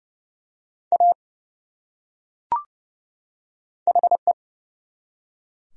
Morse code, heard as a keyed tone, about 30 wpm, 700 Hz. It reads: A HE
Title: Top 100 Words 30wpm